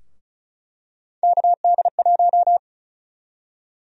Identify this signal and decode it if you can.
Morse code, heard as a keyed tone, about 36 wpm, 700 Hz. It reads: KD1
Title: USA Call Sign Prefixes - 35wpm